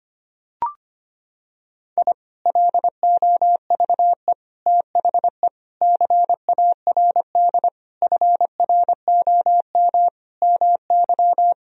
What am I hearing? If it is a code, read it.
Morse code, heard as a keyed tone, about 25 wpm, 700 Hz. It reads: I LOVE THE CARD FROM MY
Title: Sentences from Top 1000 Words - Review Part 4 - 25wpm